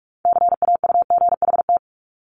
Morse code, heard as a keyed tone, about 46 wpm, 700 Hz. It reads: CAUGHT
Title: Multi-letter Phonemes 45wpm